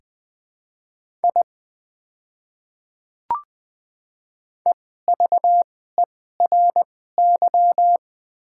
Morse code, heard as a keyed tone, about 20 wpm, spaced as for 15 wpm, 700 Hz. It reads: I EVERY
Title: 2x Character Spacing - Top 200 Words - 20wpm